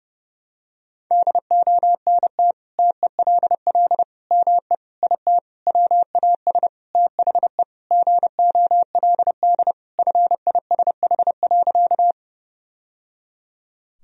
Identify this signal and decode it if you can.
Morse code, heard as a keyed tone, about 30 wpm, 700 Hz. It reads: DONT TELL ME IT WAS THE GOLD FISH.
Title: Sentences from Top 1000 Words - Review Part 3 - 30wpm